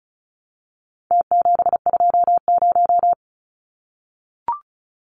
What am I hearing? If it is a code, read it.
Morse code, heard as a keyed tone, about 34 wpm, 700 Hz. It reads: T720
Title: Numbers - Contest Style - 4-Digit 35wpm